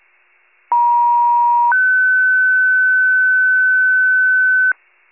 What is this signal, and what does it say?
Two-tone page, heard as a one-second tone, then a three-second tone, about 950 then 1500 Hz, with noise. It is heard from 0.7 to 4.7 s.
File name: pager / 83 RESCUE OFC / wcalert